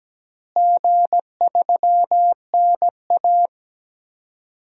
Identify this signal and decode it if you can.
Morse code, heard as a keyed tone, about 17 wpm, 700 Hz. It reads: G3NA
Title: DXCC-100 Call Signs - 4 Characters - Repeated 2x - 17wpm